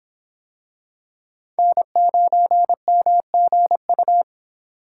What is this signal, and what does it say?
Morse code, heard as a keyed tone, about 24 wpm, 700 Hz. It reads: N9MGU